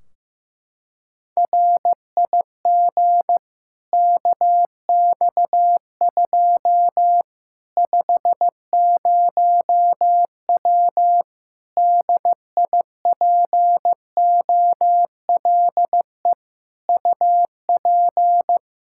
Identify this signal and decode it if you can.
Morse code, heard as a keyed tone, about 15 wpm, 700 Hz. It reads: RIG KX2 50W DIPOLE UP
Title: K1USN SST QSO Practice 15wpm